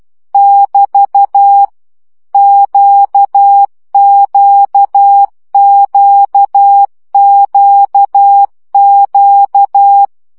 Morse code, heard as a keyed tone, about 12 wpm, 800 Hz. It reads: <BT> QQQQQ
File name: CW gyakorlas q 10 baud